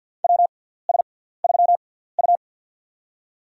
Morse code, heard as a keyed tone, about 47 wpm, spaced as for 17 wpm, 700 Hz. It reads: WS3U
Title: USA Call Signs - 6x Character Spacing - 50wpm